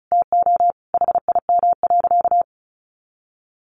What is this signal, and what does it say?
Morse code, heard as a keyed tone, about 34 wpm, 700 Hz. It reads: TO HIM.
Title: Sentences from Top 1000 Words 35wpm